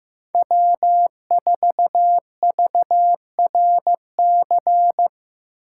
Morse code, heard as a keyed tone, about 18 wpm, 700 Hz. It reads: W4VRC
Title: Speed Racing - USA 1x3 Call Signs 15wpm